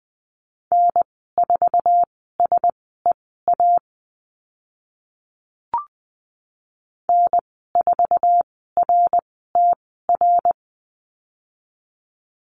Morse code, heard as a keyed tone, about 20 wpm, spaced as for 15 wpm, 700 Hz. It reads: N4SEA N4RTR
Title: 076 Lesson 13g - Letter L - Call Signs 20wpm